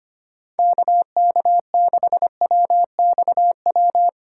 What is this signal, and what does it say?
Morse code, heard as a keyed tone, about 25 wpm, 700 Hz. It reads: KK6WXW
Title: USA Callsign DE Callsign - 6 Characters - 25wpm